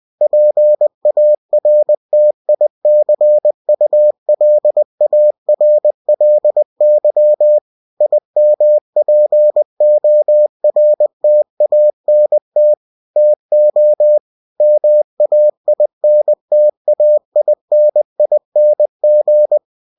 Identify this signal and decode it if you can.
Morse code, heard as a keyed tone, about 20 wpm, 600 Hz. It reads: PARTICULARLY IMPORTANT TO MAINTAINING
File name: cw-books / winnie-the-pooh / winnie-20wpm-page-0121